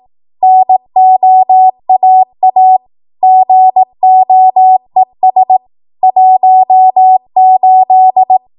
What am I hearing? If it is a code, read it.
Morse code, heard as a keyed tone, about 18 wpm, 750 Hz. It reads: NOAA GOES 18
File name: Morse / 18WPM